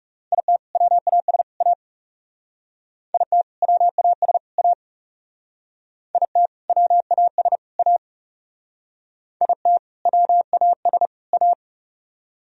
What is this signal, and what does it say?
Morse code, heard as a keyed tone, about 36 wpm, 700 Hz. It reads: IT WAS A IT WAS A IT WAS A IT WAS A
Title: Speed-Racing - Sets of 3 Words 30wpm